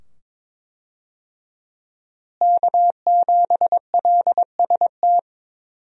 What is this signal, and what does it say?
Morse code, heard as a keyed tone, about 21 wpm, 700 Hz. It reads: K7LST
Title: USA Call Signs - 5 Characters - Repeated 2x - 22wpm